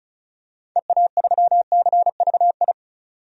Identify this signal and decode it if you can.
Morse code, heard as a keyed tone, about 36 wpm, 700 Hz. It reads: EA3CVI
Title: DXCC-100 Call Signs - 6 Characters - 35wpm